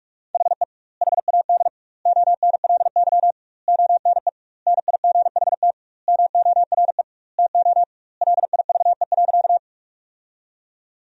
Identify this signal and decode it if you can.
Morse code, heard as a keyed tone, about 46 wpm, 700 Hz. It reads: HE HAD ONLY ONE NIGHT MORE TO LIVE.
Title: 2x Word-Spacing - Sentences from Top 300 Words - 45wpm